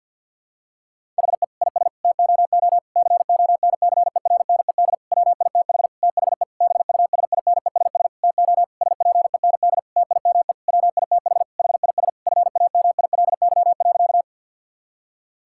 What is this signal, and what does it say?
Morse code, heard as a keyed tone, about 50 wpm, 700 Hz. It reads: HE IS TOO CONCERNED WITH THE BUSINESS TO SPEND TIME WITH HIS FAMILY.
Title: Sentences from Top 1000 Words - Review Part 6 - 50wpm